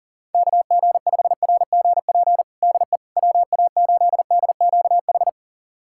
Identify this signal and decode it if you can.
Morse code, heard as a keyed tone, about 37 wpm, 700 Hz. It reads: KG5RGP DE WA8DQH
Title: USA Callsign DE Callsign 40wpm